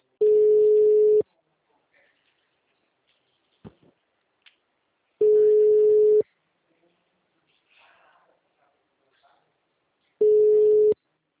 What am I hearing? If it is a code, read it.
Ringback tone (Europe, 425 Hz), with speech.
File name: out- -2003- - - .37272